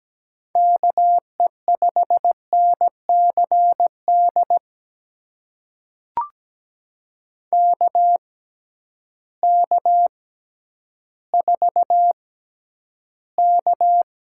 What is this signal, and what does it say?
Morse code, heard as a keyed tone, about 17 wpm, spaced as for 10 wpm, 700 Hz. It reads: KE5NCD KK4K
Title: USA Call Signs - 6 Characters - 6x Character Spacing - 17wpm